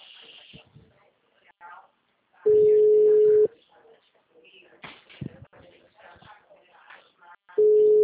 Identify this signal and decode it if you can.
Ringback tone (Europe, 425 Hz), with speech.
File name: out- -2035- - - .10497